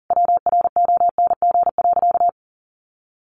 Morse code, heard as a keyed tone, about 40 wpm, 700 Hz. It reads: WRONG.
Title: A Sound Of Thunder 40wpm